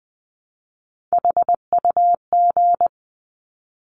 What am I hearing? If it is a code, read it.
Morse code, heard as a keyed tone, about 20 wpm, 700 Hz. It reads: HUG
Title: 260 Lesson 42e - Words 20-20wpm